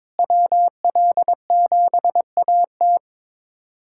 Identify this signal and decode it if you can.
Morse code, heard as a keyed tone, about 22 wpm, 700 Hz. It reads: WL7AT
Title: USA Callsign DE Callsign 22wpm